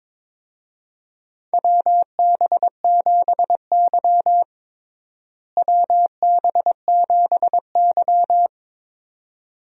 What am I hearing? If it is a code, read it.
Morse code, heard as a keyed tone, about 22 wpm, 700 Hz. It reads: WB7Y WB7Y
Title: USA Call Signs - 4 Characters - Repeated 2x - 22wpm